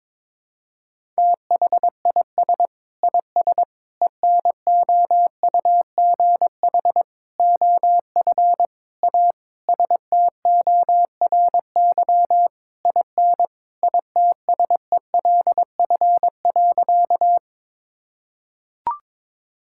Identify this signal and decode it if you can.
Morse code, heard as a keyed tone, about 22 wpm, 700 Hz. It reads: THIS IS ENOUGH OF A STORY IN ITSELF.
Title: Sentences from Top 1000 Words - Review Part 5 - 22wpm